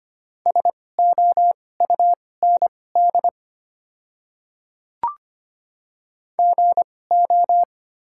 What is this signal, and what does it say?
Morse code, heard as a keyed tone, about 25 wpm, spaced as for 18 wpm, 690 Hz. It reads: SOUND GO
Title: 2x Character Spacing - Top 100 Words - 25wpm